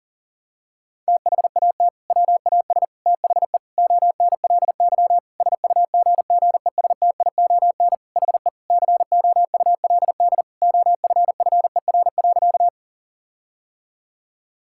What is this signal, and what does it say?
Morse code, heard as a keyed tone, about 40 wpm, 700 Hz. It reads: THAT WAS THE ONLY SUGGESTION HE COULD OFFER.
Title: Sentences from Top 1600 Words 40wpm